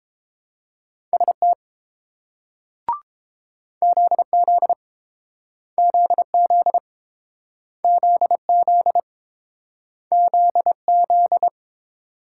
Morse code, heard as a keyed tone, about 31 wpm, 700 Hz. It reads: ST ZZ ZZ ZZ ZZ